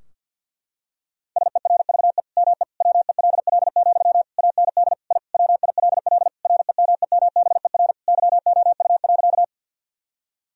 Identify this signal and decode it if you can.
Morse code, heard as a keyed tone, about 52 wpm, 700 Hz. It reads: SERVE ME WELL, AND I WILL REMEMBER YOU.
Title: Sentences from Top 1000 Words - Review Part 2 - 50wpm